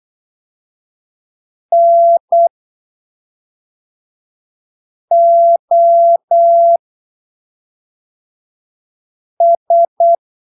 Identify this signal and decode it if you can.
Morse code, heard as a keyed tone, about 8 wpm, spaced as for 3 wpm, 680 Hz. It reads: NOS
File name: arquivos / texto / texto 05 08ppm